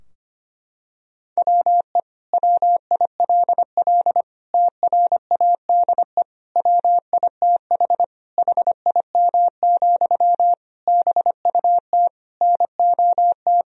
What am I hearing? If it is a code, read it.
Morse code, heard as a keyed tone, about 25 wpm, 700 Hz. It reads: WE WILL TRADE WITH HIM, BUT NOT
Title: Sentences from Top 1000 Words - Review Part 4 - 25wpm